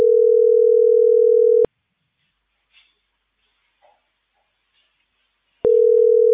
Ringback tone (North America), 440 + 480 Hz.